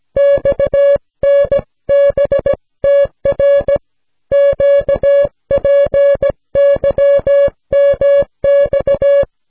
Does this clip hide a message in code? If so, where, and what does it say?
Yes: Morse code from 0.2 to 9.2 s, heard as a keyed tone, about 17 wpm, 560 Hz. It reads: XNBTR QPYMX